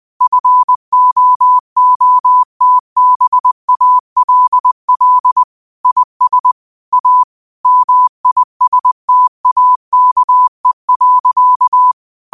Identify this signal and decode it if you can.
Morse code, heard as a keyed tone, about 20 wpm, 1000 Hz. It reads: FOOTBALL IS A MISTAKE.
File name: qotdfiles / 20wpm